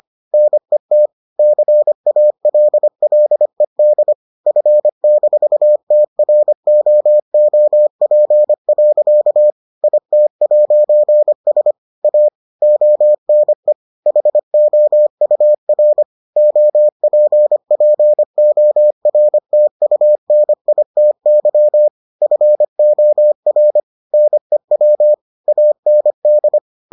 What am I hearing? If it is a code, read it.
Morse code, heard as a keyed tone, about 25 wpm, 600 Hz. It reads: NET CALLED F-TROOP. IT'S A ONE HOUR OPPORTUNITY FOR NEW AND